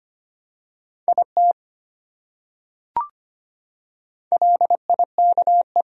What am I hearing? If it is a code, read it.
Morse code, heard as a keyed tone, about 25 wpm, 700 Hz. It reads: IT LIKE
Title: Top 100 Words Repeated 25wpm